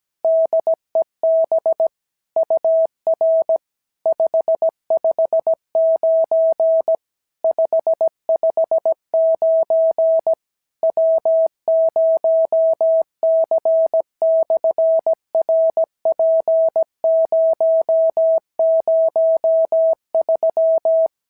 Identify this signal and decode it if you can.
Morse code, heard as a keyed tone, about 17 wpm, 650 Hz. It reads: DEB UR 559 559 W0C/RP003
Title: SOTA QSO Practice 17wpm